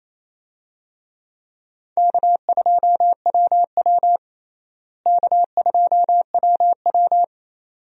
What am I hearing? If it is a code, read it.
Morse code, heard as a keyed tone, about 28 wpm, 700 Hz. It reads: K2WW K2WW
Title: USA Call Signs - 4 Characters - Repeated 2x - 28wpm